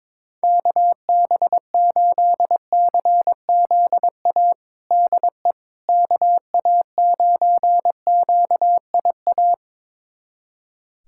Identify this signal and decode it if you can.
Morse code, heard as a keyed tone, about 22 wpm, 700 Hz. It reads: KB8CZA DE KA9QIA
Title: USA Callsign DE Callsign 22wpm